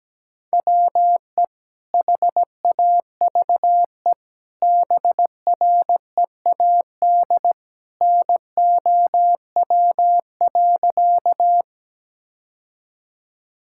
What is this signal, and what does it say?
Morse code, heard as a keyed tone, about 17 wpm, 700 Hz. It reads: WE HAVE BREAD NOW.
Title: Sentences from Top 1600 Words 17wpm